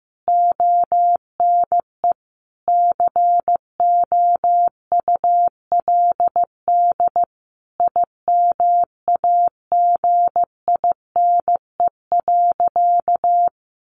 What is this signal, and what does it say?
Morse code, heard as a keyed tone, about 15 wpm, 700 Hz. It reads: ONE COULD IMAGINE.
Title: Sentences from Top 1400 Words 15wpm